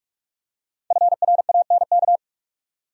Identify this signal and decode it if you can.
Morse code, heard as a keyed tone, about 46 wpm, 700 Hz. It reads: FRANK
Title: USA Common Names - Max 5 Letters - 45wpm